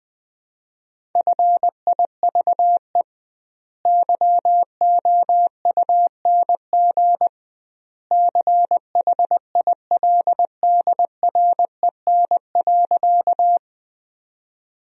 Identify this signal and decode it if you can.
Morse code, heard as a keyed tone, about 20 wpm, 700 Hz. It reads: FIVE YOUNG CHILDREN.
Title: 2x Word-Spacing - Sentences from Top 400 Words - 20wpm